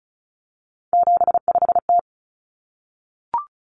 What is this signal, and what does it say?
Morse code, heard as a keyed tone, about 34 wpm, 700 Hz. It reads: 75T
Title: Common Cut Numbers - 3-Digit 35wpm